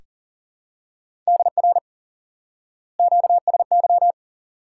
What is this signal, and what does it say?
Morse code, heard as a keyed tone, about 38 wpm, 700 Hz. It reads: DR QSY
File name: IQE Course 3 - Lesson 89 - YAGI - Two Elements - 40wpm